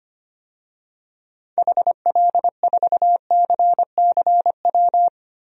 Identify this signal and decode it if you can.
Morse code, heard as a keyed tone, about 25 wpm, 700 Hz. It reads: HL4CCW
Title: DXCC-100 Call Signs - 6 Characters - Repeated 3x - 25wpm